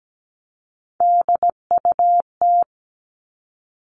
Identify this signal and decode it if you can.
Morse code, heard as a keyed tone, about 17 wpm, 700 Hz. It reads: DUT
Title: All Cut Numbers - 3-Digit 17wpm